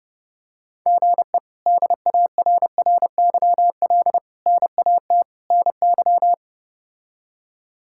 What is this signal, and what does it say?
Morse code, heard as a keyed tone, about 30 wpm, 700 Hz. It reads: GE DARRYL NAT NY
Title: K1USN SST QSO Practice - 30wpm